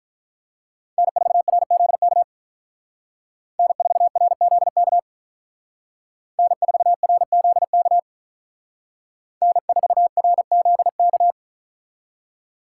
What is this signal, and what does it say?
Morse code, heard as a keyed tone, about 42 wpm, 700 Hz. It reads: N4RZK N4RZK N4RZK N4RZK
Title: Speed Racing - USA 1x3 Call Signs 35wpm